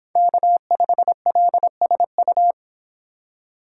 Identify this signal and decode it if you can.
Morse code, heard as a keyed tone, about 30 wpm, 700 Hz. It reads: K5LSU